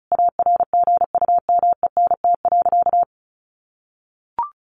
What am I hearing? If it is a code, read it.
Morse code, heard as a keyed tone, about 34 wpm, 700 Hz. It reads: ARGUMENT.
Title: Sentences from Top 1400 Words 35wpm